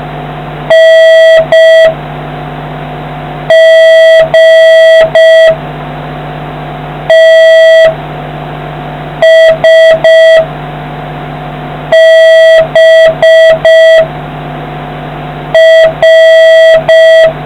Morse code, heard as a keyed tone, about 5 wpm, 1950 Hz, with noise. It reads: N G T S B R